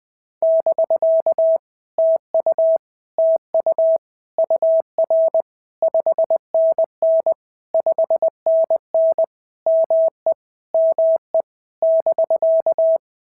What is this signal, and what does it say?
Morse code, heard as a keyed tone, about 20 wpm, 650 Hz. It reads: <BK> TU TU UR 5NN 5NN ME ME <BK>
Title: POTA QSO Practice - 20wpm